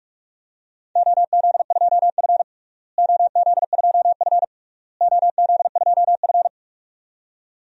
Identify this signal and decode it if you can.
Morse code, heard as a keyed tone, about 44 wpm, 700 Hz. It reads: OZ2F OZ2F OZ2F
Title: DXCC-100 Call Signs - 4 Characters - Repeated 3x - 45wpm